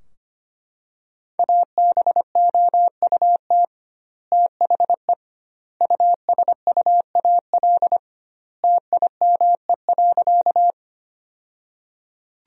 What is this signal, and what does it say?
Morse code, heard as a keyed tone, about 25 wpm, 700 Hz. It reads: ABOUT THE USUAL TIME.
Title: 2x Word-Spacing - Sentences from Top 400 Words - 25wpm